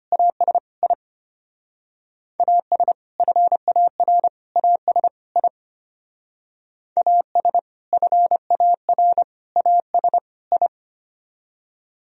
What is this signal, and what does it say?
Morse code, heard as a keyed tone, about 30 wpm, 700 Hz. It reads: AS I AS FAR AS I AS FAR AS I
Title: Speed-Racing - Sets of 4 Words 25wpm